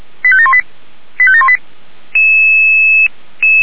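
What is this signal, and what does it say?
Five-tone selective call (ZVEI), heard as quick runs of five tones, tones of about 70 ms. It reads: 85718 85718
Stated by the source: Five-tone ZVEI call received by SDR, Feuerwehr Orlamünde.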